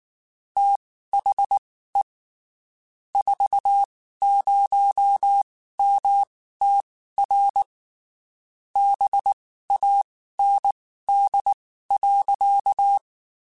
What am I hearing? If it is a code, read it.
Morse code, heard as a keyed tone, about 19 wpm, spaced as for 14 wpm, 800 Hz. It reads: THE 40MTR BAND.
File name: (spaced) / QSOs Spaced-mp3s / kc7sac @18 sp